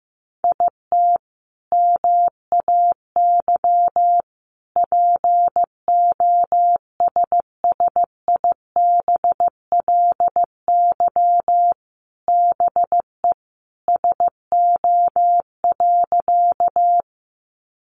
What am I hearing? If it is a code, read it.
Morse code, heard as a keyed tone, about 15 wpm, 700 Hz. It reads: IT MAY POSSIBLY BE SO.
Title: Sentences from Top 1300 Words 15wpm